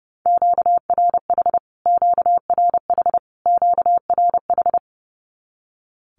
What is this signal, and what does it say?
Morse code, heard as a keyed tone, about 30 wpm, 700 Hz. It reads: QRH QRH QRH
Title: Q-Codes Repeated 3x 30wpm